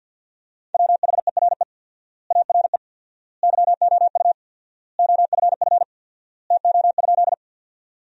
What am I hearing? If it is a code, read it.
Morse code, heard as a keyed tone, about 50 wpm, 700 Hz. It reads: WHERE ARE YOU OFF TO?
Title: 4x Word-Spacing - Sentences from Top 200 Words - 50wpm